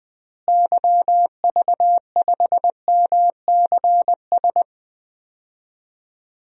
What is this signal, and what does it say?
Morse code, heard as a keyed tone, about 21 wpm, 700 Hz. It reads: YV5MCS
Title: DXCC-100 Call Signs - 6 Characters - Repeated 2x - 20wpm